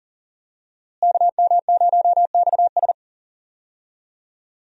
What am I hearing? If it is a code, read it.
Morse code, heard as a keyed tone, about 40 wpm, 700 Hz. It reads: KM0XS
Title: USA Call Signs - 5 Characters - Repeated 2x - 40wpm